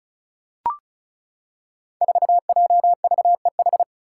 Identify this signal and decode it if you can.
Morse code, heard as a keyed tone, about 34 wpm, 700 Hz. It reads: 4JVEH
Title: Five Challenging Letter-Number Combinations 35wpm